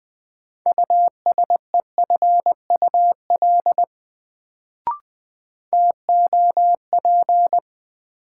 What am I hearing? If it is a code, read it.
Morse code, heard as a keyed tone, about 20 wpm, 700 Hz. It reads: USEFUL TOP
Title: Sentences from Top 700 Words - New Word Drill - 20wpm